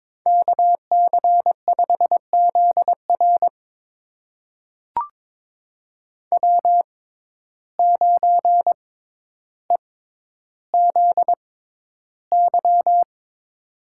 Morse code, heard as a keyed tone, about 22 wpm, spaced as for 11 wpm, 700 Hz. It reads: KC5ZR W9EZY
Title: USA Call Signs - 5 Characters - 6x Character Spacing - 22wpm